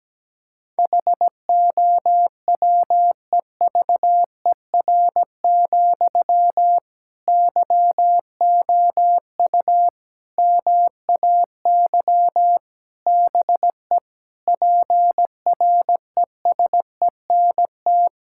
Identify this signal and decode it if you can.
Morse code, heard as a keyed tone, about 17 wpm, 700 Hz. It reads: HOWEVER, YOU MAY BE PRESENT
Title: Sentences from Top 1600 Words 17wpm